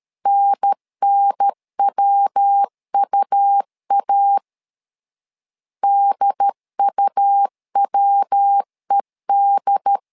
Morse code, heard as a keyed tone, about 13 wpm, 800 Hz. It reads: NNWUA DUWED
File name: LEZIONI / 3 MWRO / 3 MW 12wpm 1633